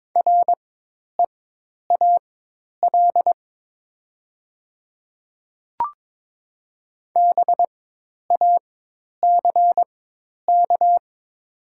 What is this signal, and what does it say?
Morse code, heard as a keyed tone, about 22 wpm, spaced as for 10 wpm, 700 Hz. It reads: REAL BACK
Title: 4x Character Spacing - Top 500 Words - 22wpm